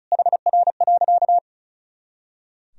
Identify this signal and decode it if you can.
Morse code, heard as a keyed tone, about 36 wpm, 700 Hz. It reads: HR.
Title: Sentences from Top 500 Words and Abbreviations 35wpm